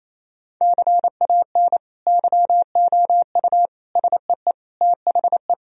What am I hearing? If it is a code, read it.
Morse code, heard as a keyed tone, about 28 wpm, 700 Hz. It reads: CAN YOU SEE THE
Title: Sentences from Top 1000 Words - Review Part 1 - 28wpm